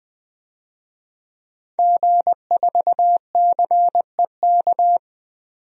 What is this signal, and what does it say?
Morse code, heard as a keyed tone, about 20 wpm, 700 Hz. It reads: G4CEK
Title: DXCC-100 Call Signs - 5 Characters - 20wpm